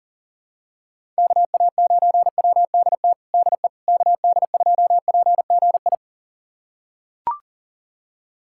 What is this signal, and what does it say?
Morse code, heard as a keyed tone, about 40 wpm, 700 Hz. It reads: KA9WDT DE KD2PGI
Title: USA Callsign DE Callsign - 6 Characters - 40wpm